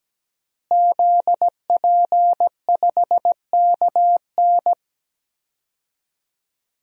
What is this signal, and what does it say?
Morse code, heard as a keyed tone, about 17 wpm, 700 Hz. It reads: ZP5KN
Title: DXCC-100 Call Signs - 5 Characters - Repeated 2x - 17wpm